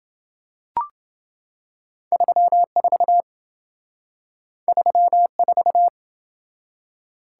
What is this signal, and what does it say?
Morse code, heard as a keyed tone, about 27 wpm, 700 Hz. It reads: 34 34